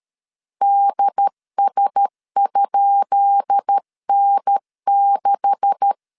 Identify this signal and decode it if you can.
Morse code, heard as a keyed tone, about 13 wpm, 800 Hz. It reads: DS?N6
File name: LEZIONI / 10 SIGNS / 10B INT 12wpm 2414